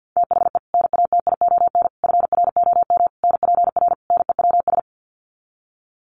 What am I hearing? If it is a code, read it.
Morse code, heard as a keyed tone, about 48 wpm, 700 Hz. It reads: THE NATION FROM NPR NEWS
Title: Sets of 5 Words 50wpm